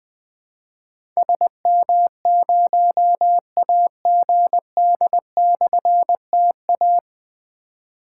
Morse code, heard as a keyed tone, about 21 wpm, 700 Hz. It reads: SM0AGD/TA
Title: Difficult Call Signs 20wpm